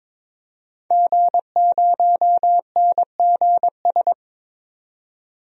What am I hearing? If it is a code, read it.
Morse code, heard as a keyed tone, about 22 wpm, 700 Hz. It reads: G0NGS
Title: DXCC-100 Call Signs - 5 Characters - 22wpm